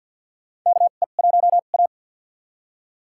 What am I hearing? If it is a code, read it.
Morse code, heard as a keyed tone, about 52 wpm, spaced as for 36 wpm, 700 Hz. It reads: KE1A
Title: USA Call Signs - 4 Characters - 2x Character Spacing - 50wpm